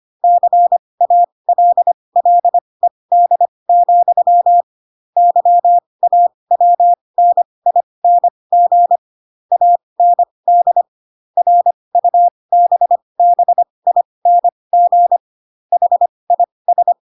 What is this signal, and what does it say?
Morse code, heard as a keyed tone, about 25 wpm, spaced as for 20 wpm, 700 Hz. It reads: CALLED, YAWNING AND RUBBING HIS
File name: Pinocchio / Pinocchio ch6 15 25 ik1hge